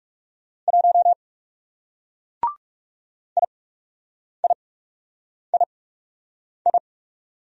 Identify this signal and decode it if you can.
Morse code, heard as a keyed tone, about 37 wpm, spaced as for 8 wpm, 700 Hz. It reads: 1 IIII
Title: Speed-Racing Single Letter-Number 30wpm